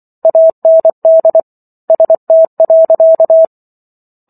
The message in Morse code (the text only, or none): AND ST.